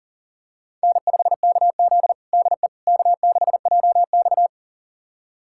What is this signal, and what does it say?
Morse code, heard as a keyed tone, about 40 wpm, 700 Hz. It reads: N5KZ DE K6JX
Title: USA Callsign DE Callsign - 4 Characters - 40wpm